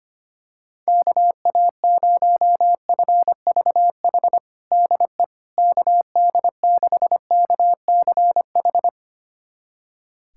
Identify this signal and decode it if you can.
Morse code, heard as a keyed tone, about 25 wpm, 700 Hz. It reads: KA0FVH DE KD6KCH
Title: USA Callsign DE Callsign - 6 Characters - 25wpm